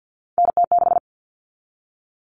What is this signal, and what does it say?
Morse code, heard as a keyed tone, about 50 wpm, 700 Hz. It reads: NT6
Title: All Cut Numbers - 3-Digit 50wpm